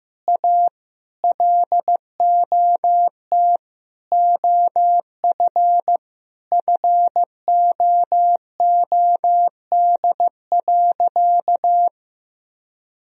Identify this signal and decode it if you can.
Morse code, heard as a keyed tone, about 16 wpm, 700 Hz. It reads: A LOT OF FOOD.
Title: Sentences from Top 1000 Words - Review Part 3 - 15wpm